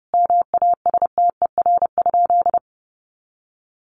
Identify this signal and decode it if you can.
Morse code, heard as a keyed tone, about 30 wpm, 700 Hz. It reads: MASTER?
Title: Sentences from Top 1300 Words 30wpm